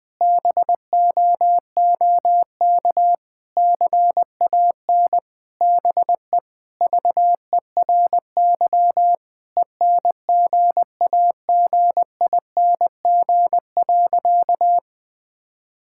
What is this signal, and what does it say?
Morse code, heard as a keyed tone, about 21 wpm, 700 Hz. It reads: BOOK CAN BE VERY ENGAGING.
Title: Sentences with 2-Word Phrases - 20wpm